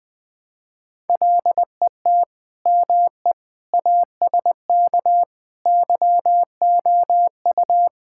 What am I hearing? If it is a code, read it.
Morse code, heard as a keyed tone, about 21 wpm, 700 Hz. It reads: LET ME ASK YOU
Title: Sentences with 4-Word Phrases - 20wpm